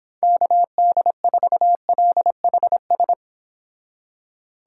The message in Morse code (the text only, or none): KD4LHS